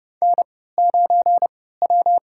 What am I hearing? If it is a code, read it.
Morse code, heard as a keyed tone, about 29 wpm, 700 Hz. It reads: N 9 W
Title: USA Call Signs - 4 Characters - 3x Character Spacing - 30wpm